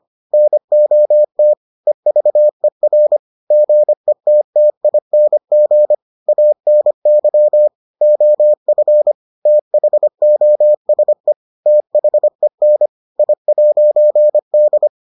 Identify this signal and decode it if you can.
Morse code, heard as a keyed tone, about 25 wpm, 600 Hz. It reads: NOT EVER GETTING ANY OF THOSE THEN I'D